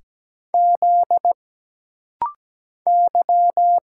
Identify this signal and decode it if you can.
Morse code, heard as a keyed tone, about 17 wpm, 700 Hz. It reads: Z Y
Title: Single Letter-Number Alt - Rapid-Fire 17wpm